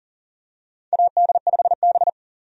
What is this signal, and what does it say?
Morse code, heard as a keyed tone, about 40 wpm, 700 Hz. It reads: AD5B
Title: USA Call Signs - 4 Characters - 40wpm